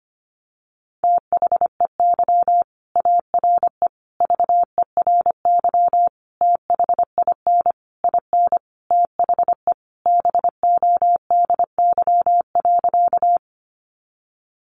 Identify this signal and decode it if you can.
Morse code, heard as a keyed tone, about 25 wpm, 700 Hz. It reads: THEY ARE VERY THIN IN THE BODY.
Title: Sentences from Top 1200 Words 25wpm